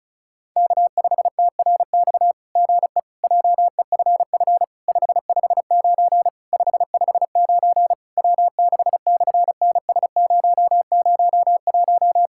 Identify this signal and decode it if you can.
Morse code, heard as a keyed tone, about 34 wpm, 700 Hz. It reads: K5TRX GE JEFF 559 559 W6/NS001
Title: SOTA QSO Practice 35wpm